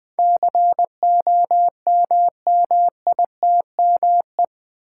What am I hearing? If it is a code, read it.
Morse code, heard as a keyed tone, about 20 wpm, 700 Hz. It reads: COMMITME
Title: Build A Word - 4 Letters At A Time - 10-Letter Words - 20wpm